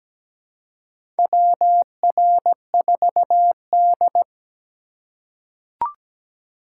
Morse code, heard as a keyed tone, about 17 wpm, 700 Hz. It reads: WR4D
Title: USA Call Signs - 4 Characters - 2x Character Spacing - 17wpm